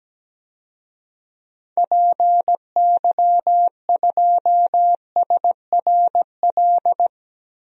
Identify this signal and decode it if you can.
Morse code, heard as a keyed tone, about 17 wpm, 700 Hz. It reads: PY2SRL
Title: DXCC-100 Call Signs - 6 Characters - Repeated 2x - 17wpm